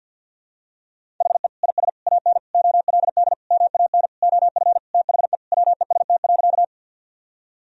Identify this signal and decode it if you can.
Morse code, heard as a keyed tone, about 51 wpm, 700 Hz. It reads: HE IS AN OLD MAN OF THE WEST.
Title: Sentences from Top 1000 Words - Review Part 2 - 50wpm